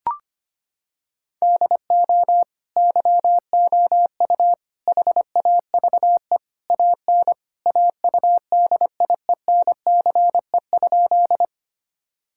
Morse code, heard as a keyed tone, about 25 wpm, 700 Hz. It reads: DO YOU HAVE AN AUDIENCE?
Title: Sentences from Top 1000 Words - Review Part 5 - 25wpm